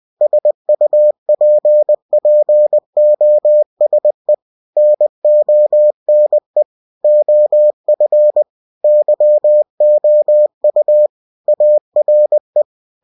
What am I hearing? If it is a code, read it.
Morse code, heard as a keyed tone, about 20 wpm, 600 Hz. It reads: SUPPOSE NONE OF YOU ARE
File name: cw-books / winnie-the-pooh / winnie-20wpm-page-0084